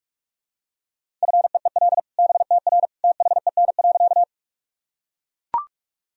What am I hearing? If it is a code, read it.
Morse code, heard as a keyed tone, about 41 wpm, 700 Hz. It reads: FEEL BTR THEN.
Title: Sentences from Top 400 Words and Abbreviations 45wpm